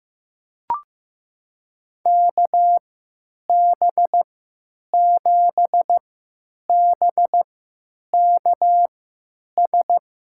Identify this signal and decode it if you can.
Morse code, heard as a keyed tone, about 15 wpm, spaced as for 9 wpm, 700 Hz. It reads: KB7BKS
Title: USA Call Signs - 6 Characters - 3x Character Spacing - 15wpm